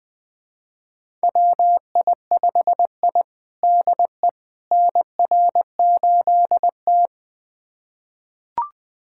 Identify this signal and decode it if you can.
Morse code, heard as a keyed tone, about 20 wpm, 700 Hz. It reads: WI5I DE NR8T
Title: USA Callsign DE Callsign - 4 Characters - 20wpm